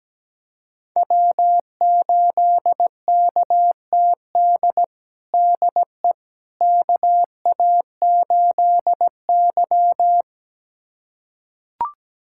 Morse code, heard as a keyed tone, about 17 wpm, 700 Hz. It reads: W8KTD DE KA8Y
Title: USA Callsign DE Callsign 17wpm